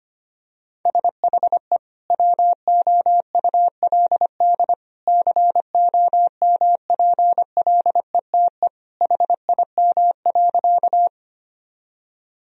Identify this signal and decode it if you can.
Morse code, heard as a keyed tone, about 25 wpm, 700 Hz. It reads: SHE WOULD COMPLETE HIM.
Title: Sentences from Top 1000 Words - Review Part 2 - 25wpm